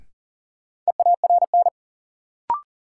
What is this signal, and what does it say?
Morse code, heard as a keyed tone, about 40 wpm, 700 Hz. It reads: EARN
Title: Sentences from Top 1500 Words - New Word Drill Alt - 40wpm